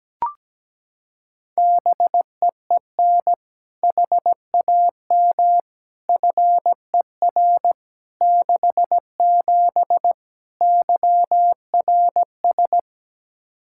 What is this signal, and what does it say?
Morse code, heard as a keyed tone, about 17 wpm, 700 Hz. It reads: BEEN HAM FER 67 YRS
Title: Practice QSO - Exchange 3 - More Info - 17wpm